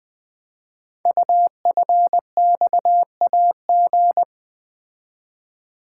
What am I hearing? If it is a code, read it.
Morse code, heard as a keyed tone, about 20 wpm, 700 Hz. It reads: UFXAG
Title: Code Groups - 5 Letters - 20wpm